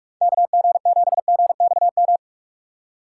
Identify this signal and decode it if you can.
Morse code, heard as a keyed tone, about 46 wpm, 700 Hz. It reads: KG7GXM
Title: USA Call Signs - 6 Characters - 2x Character Spacing - 45wpm